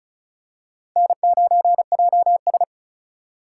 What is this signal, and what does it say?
Morse code, heard as a keyed tone, about 36 wpm, 700 Hz. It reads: N9JS
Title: USA Call Signs - 4 Characters - 4x Character Spacing - 35wpm